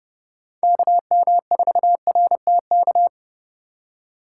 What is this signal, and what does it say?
Morse code, heard as a keyed tone, about 32 wpm, 700 Hz. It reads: KM4RTK
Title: USA Call Signs - 6 Characters - 6x Character Spacing - 30wpm